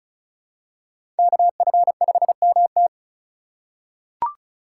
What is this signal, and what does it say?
Morse code, heard as a keyed tone, about 36 wpm, 700 Hz. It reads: KF5MT
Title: USA Call Signs - 5 Characters - 2x Character Spacing - 35wpm